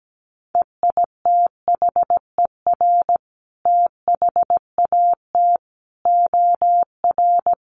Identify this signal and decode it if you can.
Morse code, heard as a keyed tone, about 17 wpm, 700 Hz. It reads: EITHER THAT OR
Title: Sentences from Top 1400 Words 17wpm